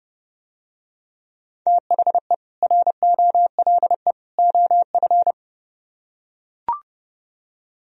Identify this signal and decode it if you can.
Morse code, heard as a keyed tone, about 30 wpm, 700 Hz. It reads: THE ROLE OF